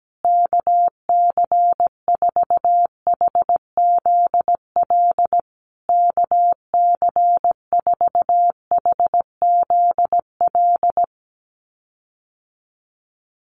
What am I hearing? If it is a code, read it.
Morse code, heard as a keyed tone, about 17 wpm, 700 Hz. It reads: KC4HZL KC4HZL
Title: USA Call Signs Repeated 3x 17wpm